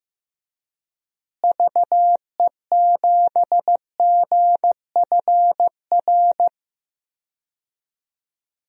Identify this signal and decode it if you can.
Morse code, heard as a keyed tone, about 16 wpm, 700 Hz. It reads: VE7GFR
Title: DXCC-100 Call Signs - 6 Characters - Repeated 3x - 15wpm